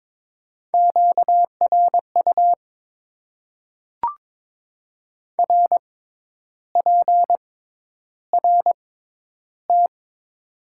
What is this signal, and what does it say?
Morse code, heard as a keyed tone, about 22 wpm, spaced as for 10 wpm, 700 Hz. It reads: QRU RPRT
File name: Top 90 QSO Elements - 6x Character Spacing - 22wpm